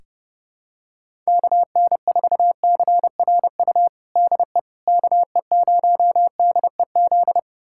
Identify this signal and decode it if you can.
Morse code, heard as a keyed tone, about 32 wpm, 700 Hz. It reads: KN4CRU DE KE0DEZ
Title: USA Callsign DE Callsign - 6 Characters - 30wpm